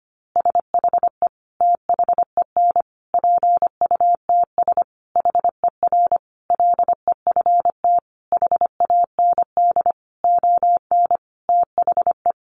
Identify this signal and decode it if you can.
Morse code, heard as a keyed tone, about 25 wpm, 700 Hz. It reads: SHE THEN PUTS HER LEFT HAND ON THE
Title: Easy Phrases 25wpm